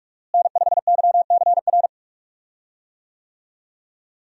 Morse code, heard as a keyed tone, about 46 wpm, 700 Hz. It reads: N5YCR
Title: USA Call Signs - 5 Characters - Repeated 2x - 45wpm